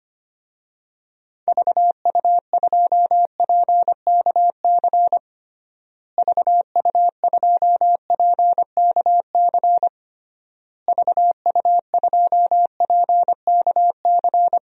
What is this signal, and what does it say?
Morse code, heard as a keyed tone, about 25 wpm, 700 Hz. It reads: VU2PKC VU2PKC VU2PKC
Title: DXCC-100 Call Signs - 6 Characters - Repeated 3x - 25wpm